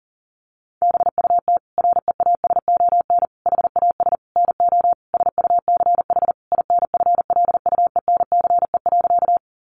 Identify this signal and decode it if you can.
Morse code, heard as a keyed tone, about 40 wpm, 700 Hz. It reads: BUT REASON HAS NO SUCH INFLUENCE.
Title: Sentences from Top 1100 Words 40wpm